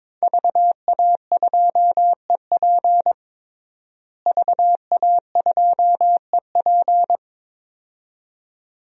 Morse code, heard as a keyed tone, about 22 wpm, 700 Hz. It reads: VA2EP VA2EP
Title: DXCC-100 Call Signs - 5 Characters - Repeated 2x - 22wpm